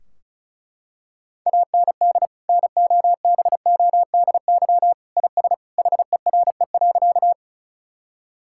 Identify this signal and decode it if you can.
Morse code, heard as a keyed tone, about 34 wpm, 700 Hz. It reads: AND NOBODY IS HERE.
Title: Sentences from Top 1600 Words 35wpm